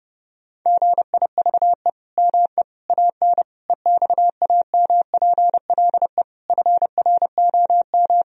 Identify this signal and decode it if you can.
Morse code, heard as a keyed tone, about 30 wpm, 700 Hz. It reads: GIVE ME AN EXAMPLE FROM
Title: Sentences from Top 1000 Words - Review Part 3 - 30wpm